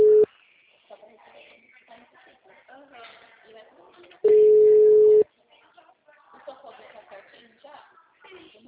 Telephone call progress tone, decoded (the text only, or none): ringback tone